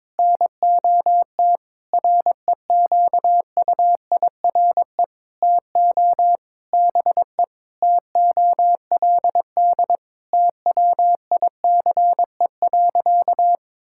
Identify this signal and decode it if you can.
Morse code, heard as a keyed tone, about 22 wpm, 700 Hz. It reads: NOT REQUIRE TO BE TOLD TWICE.
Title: Sentences from Top 1600 Words 22wpm